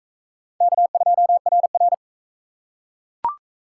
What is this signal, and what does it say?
Morse code, heard as a keyed tone, about 40 wpm, 700 Hz. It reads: K2RR